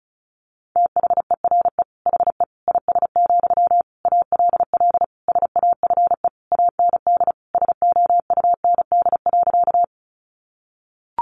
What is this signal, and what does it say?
Morse code, heard as a keyed tone, about 34 wpm, 700 Hz. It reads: THERE HE IS, ALL SAFE AND SOUND.
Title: Sentences from Top 900 Words 35wpm